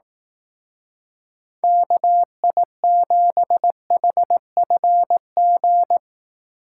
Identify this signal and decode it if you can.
Morse code, heard as a keyed tone, about 18 wpm, 700 Hz. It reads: KI7HFG
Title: Speed Racing - USA Call Signs 15wpm